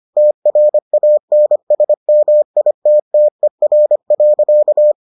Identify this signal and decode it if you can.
Morse code, heard as a keyed tone, about 25 wpm, 600 Hz. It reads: TRANSMITTER.